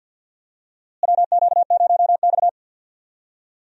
Morse code, heard as a keyed tone, about 47 wpm, 700 Hz. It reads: WQ0X
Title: USA Call Signs - 4 Characters - Repeated 3x - 50wpm